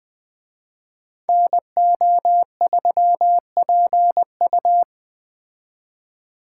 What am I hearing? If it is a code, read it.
Morse code, heard as a keyed tone, about 20 wpm, 700 Hz. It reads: NO3PU